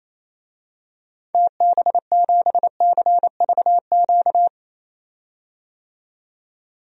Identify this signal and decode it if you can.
Morse code, heard as a keyed tone, about 28 wpm, 700 Hz. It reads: TB7CVQ
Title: DXCC-100 Call Signs - 6 Characters - 28wpm